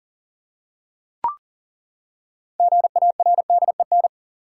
Morse code, heard as a keyed tone, about 37 wpm, 700 Hz. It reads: GARDEN
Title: Six Letter Words - 40wpm